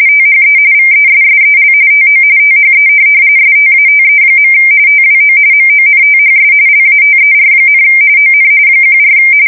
RTTY, about 45.45 baud. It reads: THE VESSEL TO THE STATE. TWO MONTHS LATER, ON 14